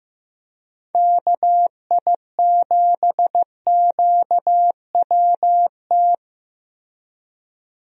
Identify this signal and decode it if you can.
Morse code, heard as a keyed tone, about 15 wpm, 700 Hz. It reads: KI7QWT
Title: USA Call Signs - 6 Characters - Repeated 2x - 15wpm